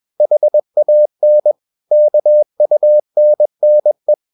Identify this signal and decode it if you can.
Morse code, heard as a keyed tone, about 21 wpm, 600 Hz. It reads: HAN KUNNE
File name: Ferie QRS 21wpm